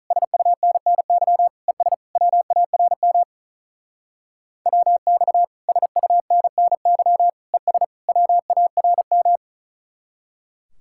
Morse code, heard as a keyed tone, about 46 wpm, 700 Hz. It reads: SUNNY ES WARM WX SUNNY ES WARM